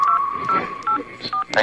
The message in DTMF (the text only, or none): 084